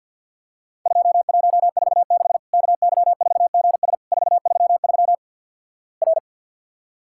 Morse code, heard as a keyed tone, about 48 wpm, 700 Hz. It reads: 2146 KY4GS 433 R
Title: Worked All Europe DX Contest - QTC Exchange - 50wpm